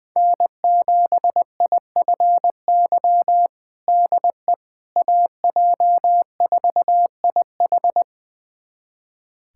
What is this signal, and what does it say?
Morse code, heard as a keyed tone, about 20 wpm, 700 Hz. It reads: N7IFY DE AJ4IH
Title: USA Callsign DE Callsign - 5 Characters - 20wpm